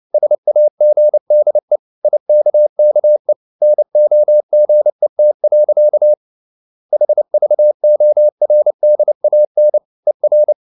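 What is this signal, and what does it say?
Morse code, heard as a keyed tone, about 29 wpm, 600 Hz. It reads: SAGDE IKKE NOGET. HVORDAN ER
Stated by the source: Fast (QRQ) Morse at 29 wpm, from CW med Gnister.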